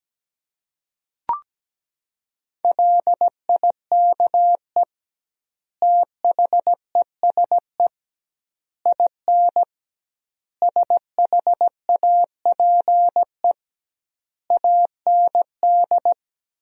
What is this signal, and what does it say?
Morse code, heard as a keyed tone, about 17 wpm, 700 Hz. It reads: LIKE THESE IN SHAPE AND
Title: 2x Word-Spacing - Sentences from Top 500 Words - 17wpm